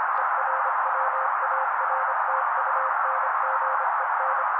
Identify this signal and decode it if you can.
Morse code, heard as a keyed tone, about 26 wpm, 530 Hz. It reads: ERWARTUNGEN
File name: daily / 25wpm noise